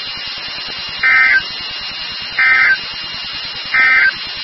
Emergency Alert System (SAME) tones, heard as harsh buzzing bursts. It is heard from 0.9 to 4.2 s.
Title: NOAA Weather Radio WXK36